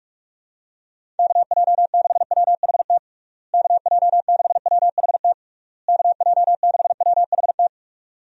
Morse code, heard as a keyed tone, about 46 wpm, 700 Hz. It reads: KJ6WHT KJ6WHT KJ6WHT
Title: USA Call Signs - 6 Characters - Repeated 3x - 45wpm